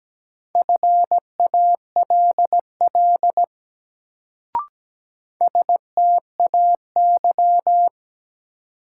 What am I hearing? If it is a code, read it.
Morse code, heard as a keyed tone, about 17 wpm, 700 Hz. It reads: FALL STAY
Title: Sentences from Top 500 Words - New Word Drill - 17wpm